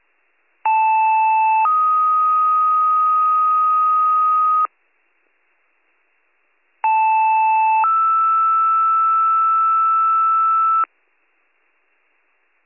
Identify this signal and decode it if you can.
Two-tone page, heard as a one-second tone, then a three-second tone, about 880 then 1250 Hz, about 880 then 1320 Hz. It is heard from 0.6 to 10.8 s.